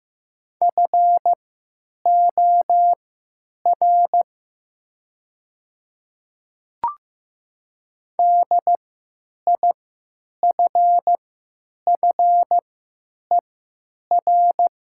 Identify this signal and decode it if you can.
Morse code, heard as a keyed tone, about 15 wpm, spaced as for 9 wpm, 700 Hz. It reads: FOR DIFFER
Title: 3x Character Spacing - Top 200 Words - 15wpm